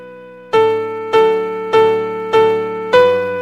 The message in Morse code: none